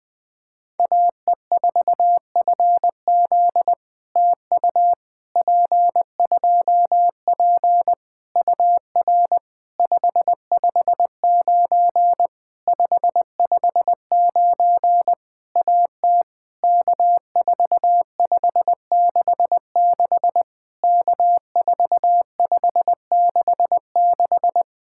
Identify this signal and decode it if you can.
Morse code, heard as a keyed tone, about 20 wpm, 700 Hz. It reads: AE4FZ TU P2P UR 559 559 AT K4566 K4566
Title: POTA QSO Practice - 20wpm